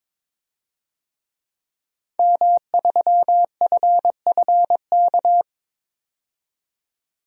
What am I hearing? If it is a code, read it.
Morse code, heard as a keyed tone, about 22 wpm, 700 Hz. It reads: M3FFK
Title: DXCC-100 Call Signs - 5 Characters - 22wpm